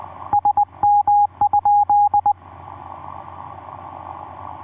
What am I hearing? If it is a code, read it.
Morse code, heard as a keyed tone, about 20 wpm, 820 Hz. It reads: SM?
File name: v5-dk1ce sq7nsp etc 12m cw 02jan2011 1320